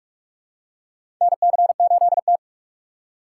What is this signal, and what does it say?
Morse code, heard as a keyed tone, about 46 wpm, 700 Hz. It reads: NC8T
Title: USA Call Signs - 4 Characters - 4x Character Spacing - 45wpm